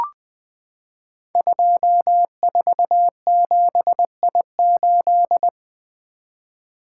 Morse code, heard as a keyed tone, about 20 wpm, 700 Hz. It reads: 247I8
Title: Code Groups - 5 Letter:Numbers - 20wpm